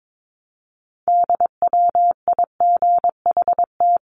Morse code, heard as a keyed tone, about 22 wpm, 700 Hz. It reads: DWIGHT
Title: CWT Exchange 22wpm